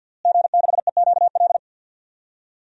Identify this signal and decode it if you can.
Morse code, heard as a keyed tone, about 52 wpm, 700 Hz. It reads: G6EQL
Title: DXCC-100 Call Signs - 5 Characters - Repeated 3x - 50wpm